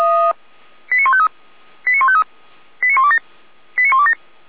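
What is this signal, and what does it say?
Five-tone selective call (ZVEI), heard as quick runs of five tones, tones of about 70 ms. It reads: 89153 89153 89127 89127
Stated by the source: ZVEI five-tone selective call 89324, received by SDR.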